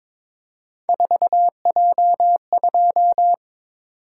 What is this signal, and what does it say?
Morse code, heard as a keyed tone, about 22 wpm, 700 Hz. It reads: 4J2
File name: practice / 22wpm / Three Letter-Number Combinations - 22wpm